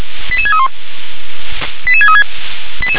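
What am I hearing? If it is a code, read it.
Five-tone selective call (ZVEI), heard as quick runs of five tones, tones of about 70 ms. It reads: 88631 88636
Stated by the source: Received on an SDR.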